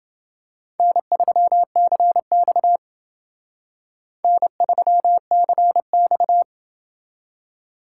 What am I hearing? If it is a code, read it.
Morse code, heard as a keyed tone, about 28 wpm, 700 Hz. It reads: N3CX N3CX